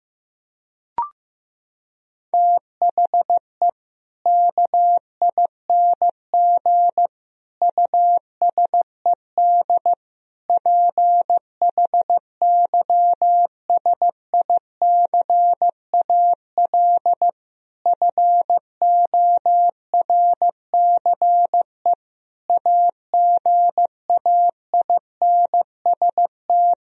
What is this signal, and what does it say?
Morse code, heard as a keyed tone, about 15 wpm, 700 Hz. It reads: THE KING USED PHYSICAL FORCE AGAINST
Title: Sentences from Top 1000 Words - Review Part 4 - 15wpm